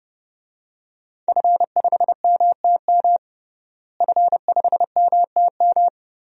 Morse code, heard as a keyed tone, about 30 wpm, 700 Hz. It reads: F5MTM F5MTM
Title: DXCC-100 Call Signs - 5 Characters - Repeated 3x - 30wpm